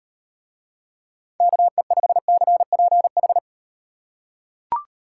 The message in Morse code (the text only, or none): KE5CPH